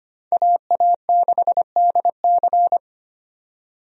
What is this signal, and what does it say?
Morse code, heard as a keyed tone, about 25 wpm, 700 Hz. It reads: AA6DC
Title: USA Call Signs - 5 Characters - 25wpm